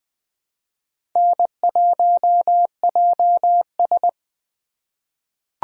Morse code, heard as a keyed tone, about 20 wpm, 700 Hz. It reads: N1JS
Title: USA Call Signs - 4 Characters - 2x Character Spacing - 20wpm